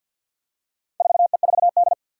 Morse code, heard as a keyed tone, about 47 wpm, 700 Hz. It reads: 4E4D
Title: Four Challenging Letter-Number Combinations 50wpm